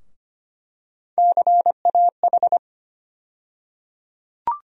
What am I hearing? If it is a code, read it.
Morse code, heard as a keyed tone, about 25 wpm, 700 Hz. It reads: CAH
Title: USA Call Sign Postfixes - 3 Letters 25wpm